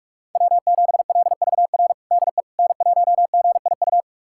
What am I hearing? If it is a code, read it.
Morse code, heard as a keyed tone, about 46 wpm, 700 Hz. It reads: W7LVR DE N1GIU
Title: USA Callsign DE Callsign - 5 Characters - 45wpm